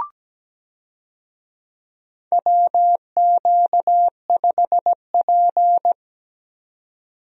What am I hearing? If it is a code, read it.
Morse code, heard as a keyed tone, about 17 wpm, 700 Hz. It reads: WQ5P
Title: USA Call Signs - 4 Characters - 17wpm